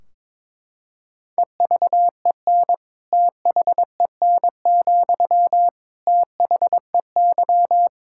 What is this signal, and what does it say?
Morse code, heard as a keyed tone, about 21 wpm, 700 Hz. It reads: EVEN THEN, THEY
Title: Sentences from Top 1600 Words 22wpm